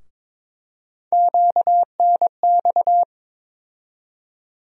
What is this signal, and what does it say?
Morse code, heard as a keyed tone, about 21 wpm, 700 Hz. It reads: QNX
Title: USA Call Sign Postfixes - 3 Letters 22wpm